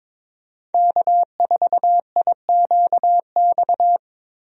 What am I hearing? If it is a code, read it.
Morse code, heard as a keyed tone, about 22 wpm, 700 Hz. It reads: K4IQX
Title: USA Call Signs - 5 Characters - 22wpm